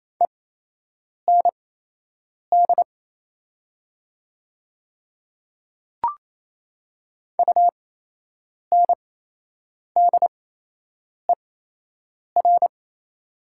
Morse code, heard as a keyed tone, about 28 wpm, spaced as for 8 wpm, 700 Hz. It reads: END UNDER
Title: 8x Character Spacing - Top 300 Words - 28wpm